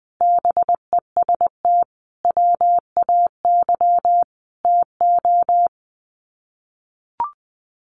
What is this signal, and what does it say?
Morse code, heard as a keyed tone, about 20 wpm, 700 Hz. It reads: BEST WAY TO
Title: Sets of 5 Words Encore 20wpm